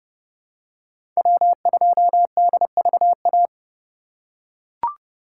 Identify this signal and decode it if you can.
Morse code, heard as a keyed tone, about 30 wpm, 700 Hz. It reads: W2DVA
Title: USA Call Signs - 5 Characters - 4x Character Spacing - 30wpm